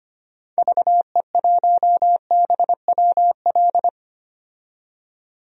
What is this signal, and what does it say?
Morse code, heard as a keyed tone, about 25 wpm, 700 Hz. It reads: VE1BWL
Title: DXCC-100 Call Signs - 6 Characters - 25wpm